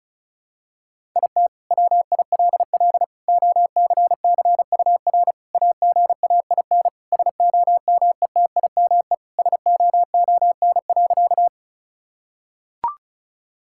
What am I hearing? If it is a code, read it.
Morse code, heard as a keyed tone, about 34 wpm, 700 Hz. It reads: IT WILL OCCUR AGAIN SOMETIME SOON.
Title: Sentences from Top 1000 Words - Review Part 6 - 35wpm